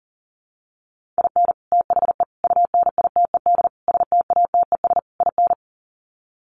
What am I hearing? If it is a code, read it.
Morse code, heard as a keyed tone, about 40 wpm, 700 Hz. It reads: IN THE UNITED STATES IN
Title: Sets of 5 Words 40wpm